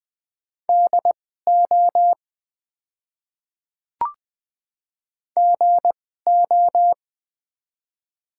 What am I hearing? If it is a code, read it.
Morse code, heard as a keyed tone, about 20 wpm, spaced as for 15 wpm, 700 Hz. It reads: DO GO
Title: 2x Character Spacing - Top 100 Words - 20wpm